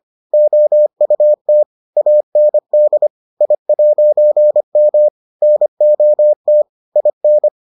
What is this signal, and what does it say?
Morse code, heard as a keyed tone, about 25 wpm, 600 Hz. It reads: OUT AND I'M NOT IN